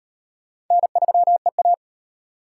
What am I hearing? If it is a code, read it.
Morse code, heard as a keyed tone, about 38 wpm, 700 Hz. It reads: N3EA